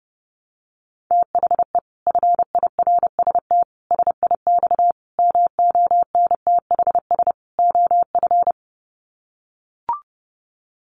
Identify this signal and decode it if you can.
Morse code, heard as a keyed tone, about 30 wpm, 700 Hz. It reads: THE FIRST SIX MONTHS OF
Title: Sets of 5 Words Encore2 30wpm